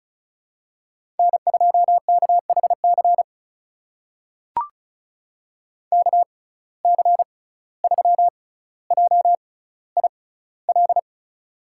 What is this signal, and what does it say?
Morse code, heard as a keyed tone, about 34 wpm, spaced as for 17 wpm, 700 Hz. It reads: N2KHC KC3JIL
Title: USA Call Signs - 6x Character Spacing - 35wpm